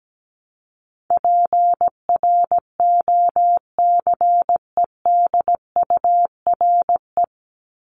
Morse code, heard as a keyed tone, about 17 wpm, 700 Hz. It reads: PROCEDURE
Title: Top 1751-2000 Words 17wpm